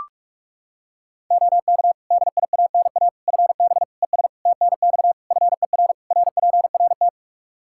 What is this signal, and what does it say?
Morse code, heard as a keyed tone, about 41 wpm, 700 Hz. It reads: OK DIANA FB ES TNX FER RPRT
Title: Practice QSO - Exchange 2 - Info - 45wpm